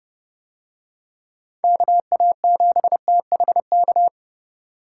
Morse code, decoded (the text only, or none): KA7THK